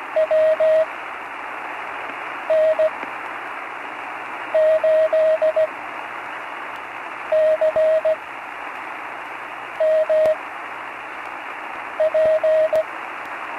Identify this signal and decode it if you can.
Morse code, heard as a keyed tone, about 16 wpm, spaced as for 5 wpm, 630 Hz, with radio noise. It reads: WN8CMP
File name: CW Learning Programs / Learning Morse Code (from Tune in the World) / 43 on the air